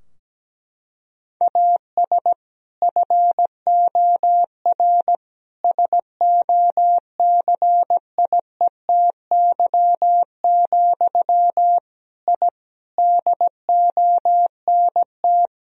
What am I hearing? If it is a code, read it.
Morse code, heard as a keyed tone, about 17 wpm, 700 Hz. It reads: AS FOR SOCIETY, I DONT
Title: Sentences from Top 1000 Words - Review Part 4 - 17wpm